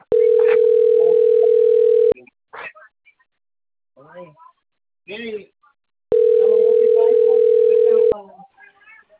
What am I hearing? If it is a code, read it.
Ringback tone (North America), 440 + 480 Hz, with speech.
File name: out- - - - - .24461